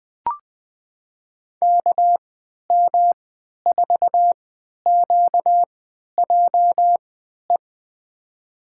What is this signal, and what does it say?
Morse code, heard as a keyed tone, about 20 wpm, spaced as for 11 wpm, 700 Hz. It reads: KM4QJE
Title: USA Call Signs - 6 Characters - 3x Character Spacing - 20wpm